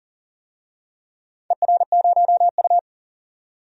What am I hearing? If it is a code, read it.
Morse code, heard as a keyed tone, about 40 wpm, 700 Hz. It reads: ER0U
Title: DXCC-100 Call Signs - 4 Characters - 40wpm